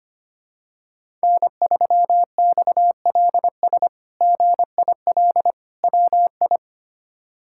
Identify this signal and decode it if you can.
Morse code, heard as a keyed tone, about 25 wpm, 700 Hz. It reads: N3XLS GIL WI
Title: K1USN SST QSO Practice - 25wpm